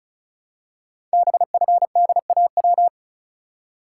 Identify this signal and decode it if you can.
Morse code, heard as a keyed tone, about 36 wpm, 700 Hz. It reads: BFDAW
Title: Code Groups - 5 Letters - 35wpm